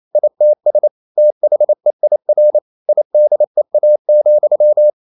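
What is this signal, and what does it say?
Morse code, heard as a keyed tone, about 28 wpm, 600 Hz. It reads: ITS THEIR IDEA,
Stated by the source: Morse code practice at 28 wpm, text of Ken Follett's novel Never in English, from CW med Gnister.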